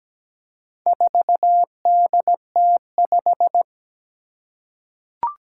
Morse code, heard as a keyed tone, about 17 wpm, 700 Hz. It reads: 4DT5
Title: Four Challenging Letter-Number Combinations 17wpm